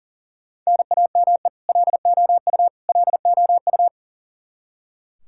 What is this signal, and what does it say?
Morse code, heard as a keyed tone, about 40 wpm, 700 Hz. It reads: NAME LOU LOU
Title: Practice QSO - Exchange 1 - Beginning - 40wpm